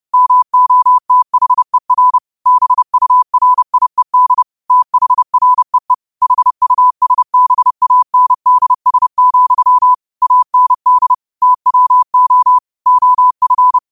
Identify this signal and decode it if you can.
Morse code, heard as a keyed tone, about 30 wpm, 1000 Hz. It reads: MOTHER BURIED THREE HUSBANDS, AND TWO OF
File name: qotdfiles / 30wpm